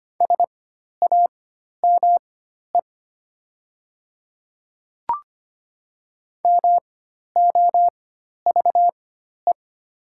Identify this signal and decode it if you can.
Morse code, heard as a keyed tone, about 25 wpm, spaced as for 12 wpm, 700 Hz. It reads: SAME MOVE
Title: 4x Character Spacing - Top 200 Words - 25wpm